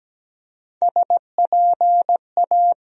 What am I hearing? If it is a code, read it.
Morse code, heard as a keyed tone, about 17 wpm, 700 Hz. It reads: SPA
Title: USA Call Sign Postfixes - 3 Letters 17wpm